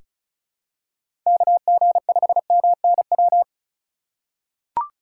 Morse code, heard as a keyed tone, about 36 wpm, 700 Hz. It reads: KG5MNW
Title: USA Call Signs - 6 Characters - 3x Character Spacing - 35wpm